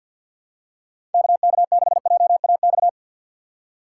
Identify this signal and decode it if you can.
Morse code, heard as a keyed tone, about 53 wpm, 700 Hz. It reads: KK6JAX
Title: USA Call Signs - 6x Character Spacing - 50wpm